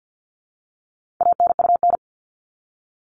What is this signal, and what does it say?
Morse code, heard as a keyed tone, about 50 wpm, 700 Hz. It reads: ANUN
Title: All Cut Numbers - 4-Digit 50wpm